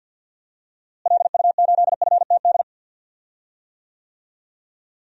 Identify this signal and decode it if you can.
Morse code, heard as a keyed tone, about 53 wpm, 700 Hz. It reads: LU8FTD
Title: DXCC-100 Call Signs - 6 Characters - Repeated 2x - 50wpm